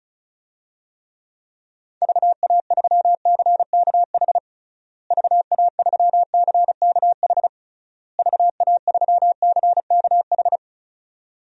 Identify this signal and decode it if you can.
Morse code, heard as a keyed tone, about 34 wpm, 700 Hz. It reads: VA3CKH VA3CKH VA3CKH
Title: DXCC-100 Call Signs - 6 Characters - Repeated 3x - 35wpm